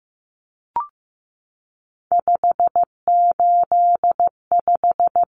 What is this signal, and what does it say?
Morse code, heard as a keyed tone, about 15 wpm, 700 Hz. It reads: 585
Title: Numbers - 3-Digit 15wpm